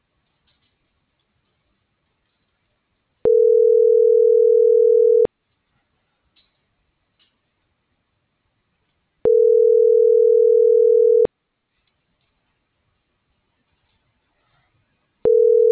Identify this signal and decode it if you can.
Ringback tone (North America), 440 + 480 Hz.